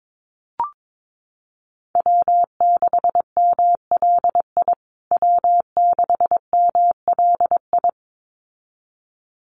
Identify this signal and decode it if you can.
Morse code, heard as a keyed tone, about 22 wpm, 700 Hz. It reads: W6MLI W6MLI
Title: USA Call Signs Repeated 2x 22wpm